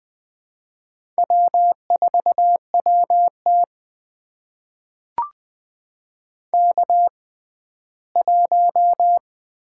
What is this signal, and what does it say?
Morse code, heard as a keyed tone, about 20 wpm, spaced as for 14 wpm, 700 Hz. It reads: W4WT K1
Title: USA Call Signs - 4 Characters - 6x Character Spacing - 20wpm